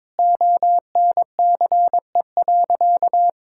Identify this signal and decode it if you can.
Morse code, heard as a keyed tone, about 21 wpm, 700 Hz. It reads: ONCE.
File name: practice / 22wpm / Sentences with 2-Word Phrases - 22wpm